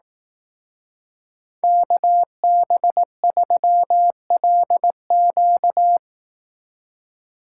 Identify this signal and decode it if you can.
Morse code, heard as a keyed tone, about 18 wpm, 700 Hz. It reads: KB3LQ
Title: Speed Racing - USA 2x2 Call Signs 15wpm